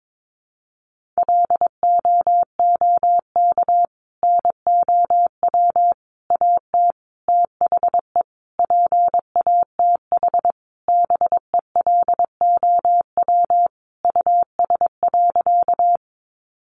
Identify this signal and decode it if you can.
Morse code, heard as a keyed tone, about 22 wpm, 700 Hz. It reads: LOOK NOW AT THE PATH BELOW US.
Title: Sentences from Top 1400 Words 22wpm